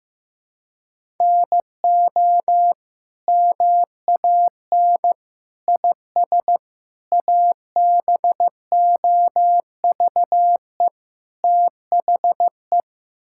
Morse code, heard as a keyed tone, about 15 wpm, 700 Hz. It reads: NO MAN IS ABOVE THE
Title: Sentences from Top 1000 Words - Review Part 5 - 15wpm